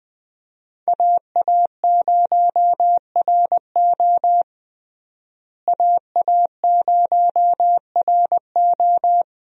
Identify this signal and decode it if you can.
Morse code, heard as a keyed tone, about 20 wpm, 700 Hz. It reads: AA0RO AA0RO
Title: USA Call Signs - 5 Characters - Repeated 3x - 20wpm